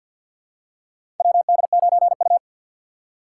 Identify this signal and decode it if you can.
Morse code, heard as a keyed tone, about 50 wpm, 700 Hz. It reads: WD9U
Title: USA Call Signs - 3x Character Spacing - 50wpm